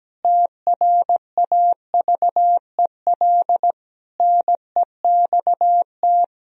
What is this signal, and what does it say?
Morse code, heard as a keyed tone, about 17 wpm, 700 Hz. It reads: TRAVEL NEXT
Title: Sentences from Top 1000 Words - Review Part 2 - 17wpm